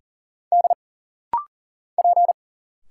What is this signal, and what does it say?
Morse code, heard as a keyed tone, about 40 wpm, 700 Hz. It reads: D P
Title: Single Letters - Rapid-Fire 40wpm